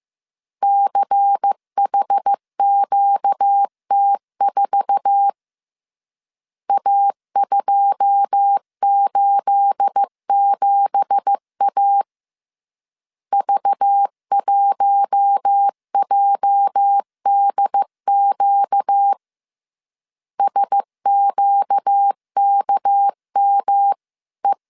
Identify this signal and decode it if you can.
Morse code, heard as a keyed tone, about 15 wpm, 800 Hz. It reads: CHQT4 A287A V1JDQ SQKME
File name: CW / DET / 1 9 1500C S A 14wpm 3222